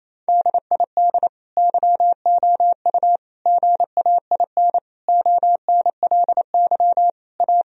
Morse code, heard as a keyed tone, about 28 wpm, 700 Hz. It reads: DID YOU GAIN ONLY A
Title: Sentences from Top 1000 Words - Review Part 2 - 28wpm